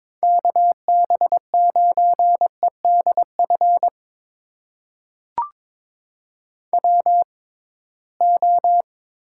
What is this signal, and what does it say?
Morse code, heard as a keyed tone, about 22 wpm, spaced as for 17 wpm, 700 Hz. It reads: KB9EDF WO
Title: USA Call Signs - 6x Character Spacing - 22wpm